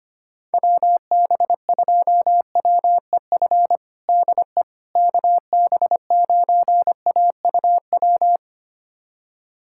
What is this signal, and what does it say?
Morse code, heard as a keyed tone, about 25 wpm, 700 Hz. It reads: WB2WEF DE KB9AUW
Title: USA Callsign DE Callsign 25wpm